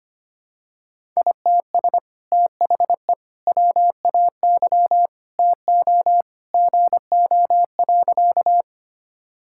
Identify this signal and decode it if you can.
Morse code, heard as a keyed tone, about 25 wpm, 700 Hz. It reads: ITS THE WAY TO GO.
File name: practice / 25wpm / Sentences with 2-Word Phrases - 25wpm